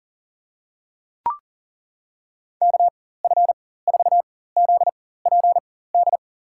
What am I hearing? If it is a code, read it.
Morse code, heard as a keyed tone, about 40 wpm, spaced as for 19 wpm, 700 Hz. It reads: KF4ZPD
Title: USA Call Signs - 4x Character Spacing - 40wpm